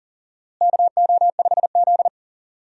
Morse code, heard as a keyed tone, about 40 wpm, 700 Hz. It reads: KO5Z
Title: USA Call Signs - 4 Characters - Repeated 3x - 40wpm